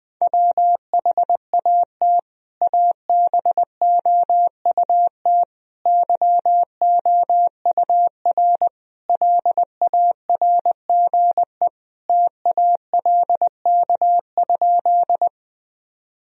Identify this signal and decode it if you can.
Morse code, heard as a keyed tone, about 20 wpm, 700 Hz. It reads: WHAT ABOUT YOUR LARGE TALK?
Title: Sentences from Top 1000 Words - Review Part 1 - 20wpm